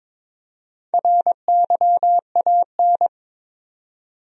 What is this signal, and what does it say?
Morse code, heard as a keyed tone, about 22 wpm, 700 Hz. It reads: RYAN
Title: USA Common Names - Max 4 Letters - 22wpm